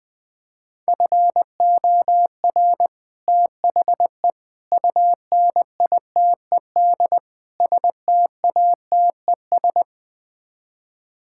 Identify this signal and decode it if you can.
Morse code, heard as a keyed tone, about 20 wpm, 700 Hz. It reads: FOR THE UNITED STATES
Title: Build A Phrase - 4-Word Phrases - 20wpm